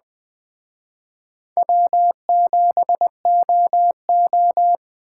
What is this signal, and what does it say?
Morse code, heard as a keyed tone, about 20 wpm, 700 Hz. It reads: W7OO
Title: Speed Racing - USA 1x2 Call Signs 17wpm